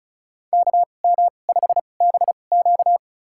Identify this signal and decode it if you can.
Morse code, heard as a keyed tone, about 36 wpm, spaced as for 25 wpm, 700 Hz. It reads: KM5BQ
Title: USA Call Signs - 5 Characters - 2x Character Spacing - 35wpm